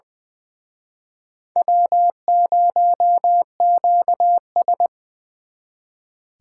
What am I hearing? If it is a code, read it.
Morse code, heard as a keyed tone, about 25 wpm, 700 Hz. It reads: W0QS
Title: Speed Racing - USA 1x2 Call Signs 20wpm